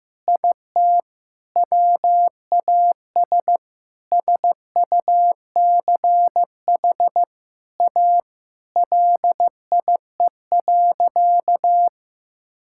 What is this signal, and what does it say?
Morse code, heard as a keyed tone, about 15 wpm, 700 Hz. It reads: IT WAS SUCH A LIE.
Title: Sentences from Top 1600 Words 15wpm